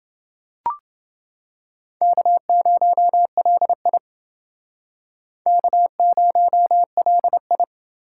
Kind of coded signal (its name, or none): Morse code